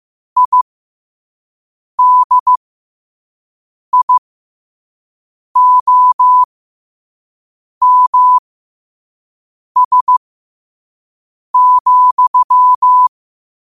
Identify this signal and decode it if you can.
Morse code, heard as a keyed tone, about 15 wpm, spaced as for 5 wpm, 1000 Hz. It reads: IDIOMS,
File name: qotdfiles / 5wpm